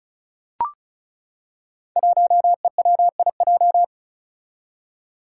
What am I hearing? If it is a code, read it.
Morse code, heard as a keyed tone, about 36 wpm, 700 Hz. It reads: 1EWIJ
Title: Five Challenging Letter-Number Combinations 35wpm